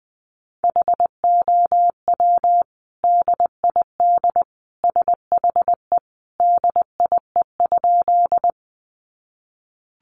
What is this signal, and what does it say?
Morse code, heard as a keyed tone, about 20 wpm, 700 Hz. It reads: HOW DID SHE DIE?
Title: Sentences from Top 1300 Words 20wpm